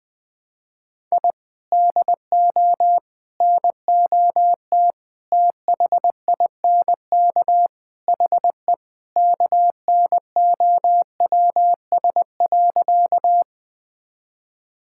Morse code, heard as a keyed tone, about 20 wpm, 700 Hz. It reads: I DO NOT THINK HE KNOWS.
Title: Sentences with 4-Word Phrases - Spoken Beginning - 20wpm